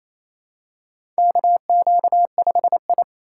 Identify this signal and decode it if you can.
Morse code, heard as a keyed tone, about 27 wpm, 700 Hz. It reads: KQ5I
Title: USA Call Signs - 4 Characters - 4x Character Spacing - 28wpm